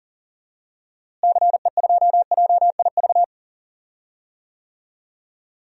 Morse code, heard as a keyed tone, about 40 wpm, 700 Hz. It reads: CE2JIV
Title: DXCC-100 Call Signs - 6 Characters - Repeated 3x - 40wpm